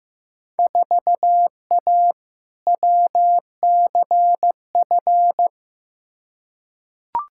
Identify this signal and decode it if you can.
Morse code, heard as a keyed tone, about 15 wpm, 700 Hz. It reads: 4A WCF
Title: Field Day Exchanges - 15wpm